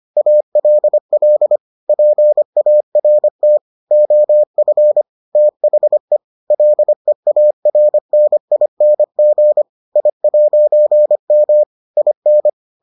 Morse code, heard as a keyed tone, about 25 wpm, 600 Hz. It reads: ALL PART OF THE LEARNING I'M IN